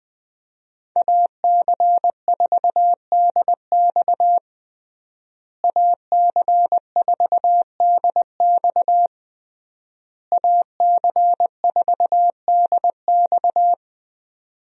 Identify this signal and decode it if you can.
Morse code, heard as a keyed tone, about 20 wpm, 700 Hz. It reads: AC4DX AC4DX AC4DX
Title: USA Call Signs - 5 Characters - Repeated 3x - 20wpm